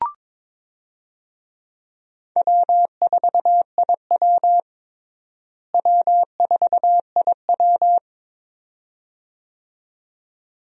Morse code, heard as a keyed tone, about 22 wpm, 700 Hz. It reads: W4IW W4IW
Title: USA Call Signs - 4 Characters - Repeated 2x - 22wpm